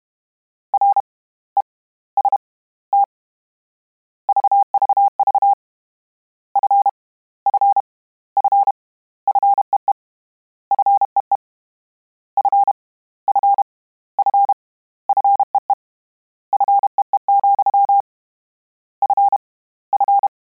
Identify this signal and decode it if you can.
Morse code, heard as a keyed tone, about 32 wpm, spaced as for 24 wpm, 800 Hz. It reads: REST VVV F F F FEE FEE F F F FEE FEE, F F
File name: Combinations / Combinations-mp3s / Combinations@30